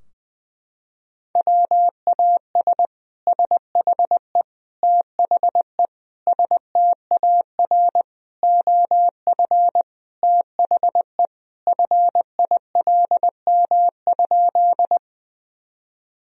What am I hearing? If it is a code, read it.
Morse code, heard as a keyed tone, about 20 wpm, 700 Hz. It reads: WAS SHE THE STAR OF THE FILM?
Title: Sentences from Top 1000 Words - Review Part 3 - 20wpm